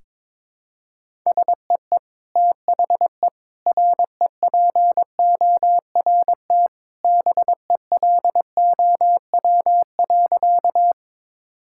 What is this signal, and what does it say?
Morse code, heard as a keyed tone, about 22 wpm, 700 Hz. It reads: SEE THE REPORT BELOW.
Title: Sentences from Top 1000 Words - Review Part 6 - 22wpm